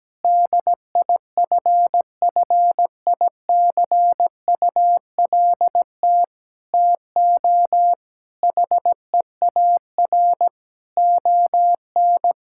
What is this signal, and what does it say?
Morse code, heard as a keyed tone, about 17 wpm, 700 Hz. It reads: DIFFICULT TO HEAR ON
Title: Sentences from Top 1000 Words - Review Part 6 - 17wpm